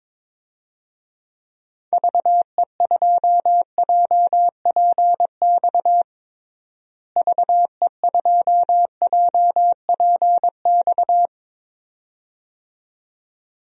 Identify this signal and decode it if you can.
Morse code, heard as a keyed tone, about 21 wpm, 700 Hz. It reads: VE2JPX VE2JPX
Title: DXCC-100 Call Signs - 6 Characters - Repeated 2x - 22wpm